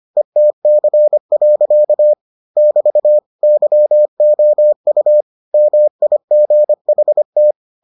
Morse code, heard as a keyed tone, about 25 wpm, 600 Hz. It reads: ETC. <BT> YOU MIGHT